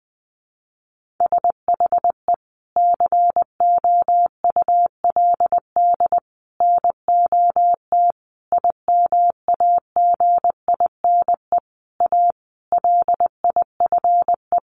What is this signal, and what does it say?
Morse code, heard as a keyed tone, about 20 wpm, 700 Hz. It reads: SHE COULD NOT IMAGINE A LIFE
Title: Sentences from Top 1400 Words 20wpm